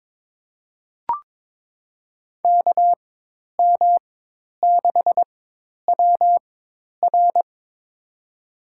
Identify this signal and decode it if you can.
Morse code, heard as a keyed tone, about 22 wpm, spaced as for 10 wpm, 700 Hz. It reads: KM6WR
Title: USA Call Signs - 5 Characters - 4x Character Spacing - 22wpm